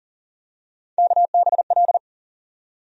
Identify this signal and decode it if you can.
Morse code, heard as a keyed tone, about 40 wpm, 700 Hz. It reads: KBL
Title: USA Call Sign Postfixes - 3 Letters 40wpm